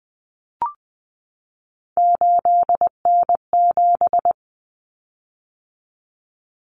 Morse code, heard as a keyed tone, about 20 wpm, 700 Hz. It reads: 8N7
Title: Common Cut Numbers - 3-Digit 20wpm